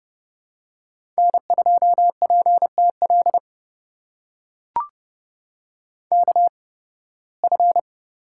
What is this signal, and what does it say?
Morse code, heard as a keyed tone, about 30 wpm, spaced as for 20 wpm, 700 Hz. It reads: N2PTL KF
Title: USA Call Signs - 5 Characters - 8x Character Spacing - 30wpm